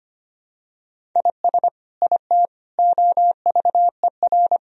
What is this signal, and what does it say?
Morse code, heard as a keyed tone, about 25 wpm, 700 Hz. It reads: IS IT OVER
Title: Sentences from Top 1000 Words - Review Part 1 - 25wpm